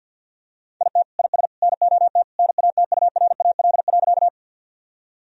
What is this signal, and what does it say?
Morse code, heard as a keyed tone, about 50 wpm, 700 Hz. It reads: IT IS NOT NATURAL.
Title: Sentences from Top 1000 Words - Review Part 4 - 50wpm